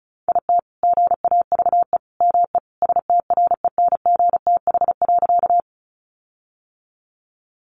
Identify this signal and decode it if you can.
Morse code, heard as a keyed tone, about 34 wpm, 700 Hz. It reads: IT GAVE ME STRENGTH.
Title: Sentences from Top 1300 Words 35wpm